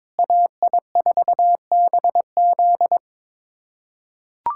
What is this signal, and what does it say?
Morse code, heard as a keyed tone, about 22 wpm, 700 Hz. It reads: AI4BZ
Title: USA Callsign DE Callsign 22wpm